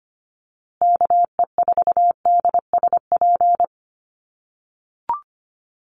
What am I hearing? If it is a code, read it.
Morse code, heard as a keyed tone, about 25 wpm, 700 Hz. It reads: KE4DSP
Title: USA Call Signs Repeated 3x 25wpm